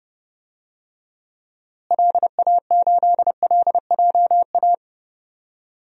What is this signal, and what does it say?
Morse code, heard as a keyed tone, about 30 wpm, 700 Hz. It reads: LA8LJA
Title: DXCC-100 Call Signs - 6 Characters - Repeated 2x - 30wpm